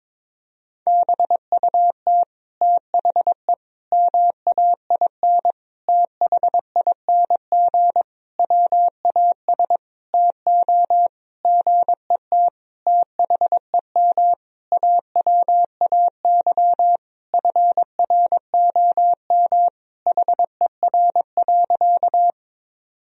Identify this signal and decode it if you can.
Morse code, heard as a keyed tone, about 22 wpm, 700 Hz. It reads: BUT THE MAIN THING WAS TO GET THEM AWAY FROM HER.
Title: Sentences from Top 1000 Words - Review Part 2 - 22wpm